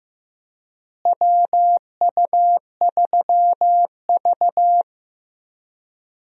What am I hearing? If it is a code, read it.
Morse code, heard as a keyed tone, about 15 wpm, 700 Hz. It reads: WU3V
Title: USA Call Signs - 4 Characters - Repeated 3x - 15wpm